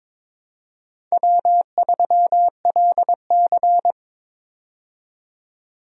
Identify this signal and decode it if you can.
Morse code, heard as a keyed tone, about 22 wpm, 700 Hz. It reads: W3LC
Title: USA Call Signs - 4 Characters - 22wpm